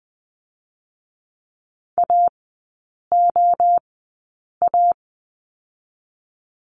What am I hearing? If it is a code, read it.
Morse code, heard as a keyed tone, about 20 wpm, 700 Hz. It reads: A O A
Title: 023 Lesson 5e - Letter o - Three Characters 20wpm